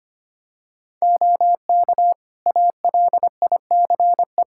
Morse code, heard as a keyed tone, about 25 wpm, 700 Hz. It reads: OK ALICE
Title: Practice QSO - Exchange 3 - More Info - 25wpm